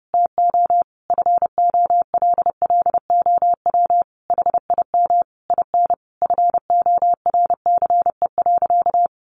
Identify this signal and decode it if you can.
Morse code, heard as a keyed tone, about 30 wpm, 700 Hz. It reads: TO FOLLOW HIM IN FORCE.
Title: Sentences from Top 700 Words 30wpm